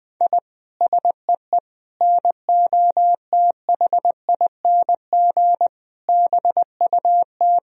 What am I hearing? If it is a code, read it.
Morse code, heard as a keyed tone, about 20 wpm, 700 Hz. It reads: I SEE NOTHING BUT
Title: Sentences from Top 1000 Words - Review Part 1 - 20wpm